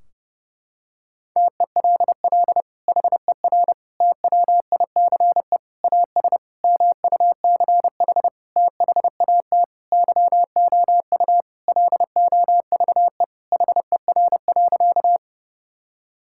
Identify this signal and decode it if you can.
Morse code, heard as a keyed tone, about 30 wpm, 700 Hz. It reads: TELL HER TWICE AS MUCH THAT YOU LOVE HER.
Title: Sentences from Top 1600 Words 30wpm